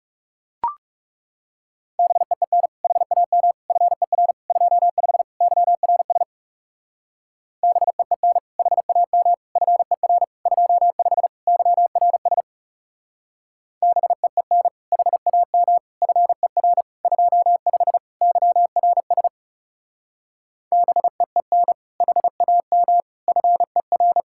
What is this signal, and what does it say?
Morse code, heard as a keyed tone, about 36 wpm, 700 Hz. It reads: BEEN HAM FER 25 YRS BEEN HAM FER 25 YRS BEEN HAM FER 25 YRS BEEN HAM FER
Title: Speed-Racing Common QSO Phrases - 30wpm